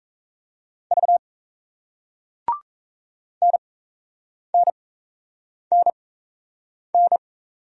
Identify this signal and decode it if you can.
Morse code, heard as a keyed tone, about 35 wpm, spaced as for 8 wpm, 700 Hz. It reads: V NNNN